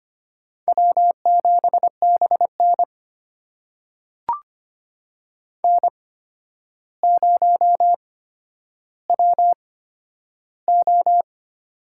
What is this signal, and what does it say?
Morse code, heard as a keyed tone, about 25 wpm, spaced as for 11 wpm, 700 Hz. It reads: W7BN N0WO
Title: USA Call Signs - 4 Characters - 8x Character Spacing - 25wpm